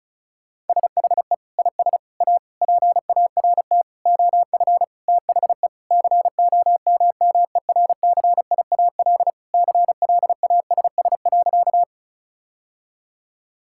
Morse code, heard as a keyed tone, about 36 wpm, 700 Hz. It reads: SHE IS A PART OF THE COMMERCIAL CLASS.
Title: Sentences from Top 1000 Words - Review Part 5 - 35wpm